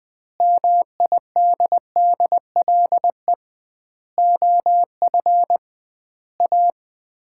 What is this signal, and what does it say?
Morse code, heard as a keyed tone, about 20 wpm, 700 Hz. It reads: MIDDLE OF A
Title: 2x Word-Spacing - Sets of 5 Words - 20wpm